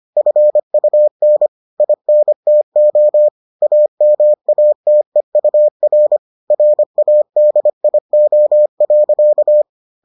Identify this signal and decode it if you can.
Morse code, heard as a keyed tone, about 25 wpm, 600 Hz. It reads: FUN INTO AMATEUR RADIO.